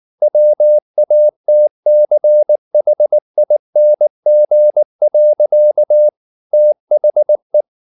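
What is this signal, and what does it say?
Morse code, heard as a keyed tone, about 19 wpm, 600 Hz. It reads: WATCHING. THE
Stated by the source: Morse code at 19 wpm from CW med Gnister, reading Ken Follett's novel Never in English.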